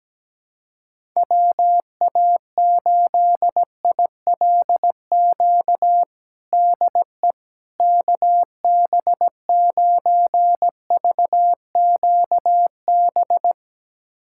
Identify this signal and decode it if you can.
Morse code, heard as a keyed tone, about 17 wpm, 700 Hz. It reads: WA8ILQ DE KB9VQB
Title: USA Callsign DE Callsign 17wpm